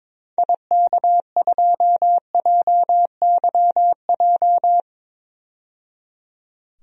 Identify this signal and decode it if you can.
Morse code, heard as a keyed tone, about 21 wpm, 700 Hz. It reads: IK2JYJ
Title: DXCC-100 Call Signs - 6 Characters - 22wpm